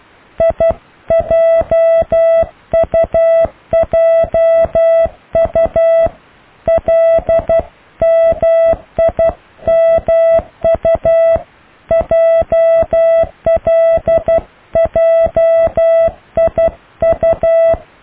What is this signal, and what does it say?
Morse code, heard as a keyed tone, about 12 wpm, 660 Hz. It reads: IJUJU LMIMU JLJIU
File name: cw / IJLMU